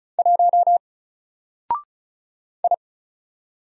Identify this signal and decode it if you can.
Morse code, heard as a keyed tone, about 35 wpm, 700 Hz. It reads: 1 I
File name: Single-Letter-Number-35wpm